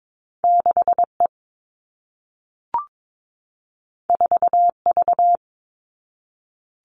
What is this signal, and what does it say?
Morse code, heard as a keyed tone, about 22 wpm, 700 Hz. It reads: 6E 4V
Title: All Cut Numbers - 2-Digit 22wpm